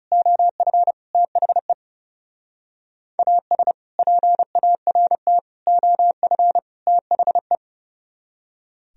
Morse code, heard as a keyed tone, about 32 wpm, 700 Hz. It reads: OF THE AS PART OF THE
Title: Speed-Racing - Sets of 4 Words 30wpm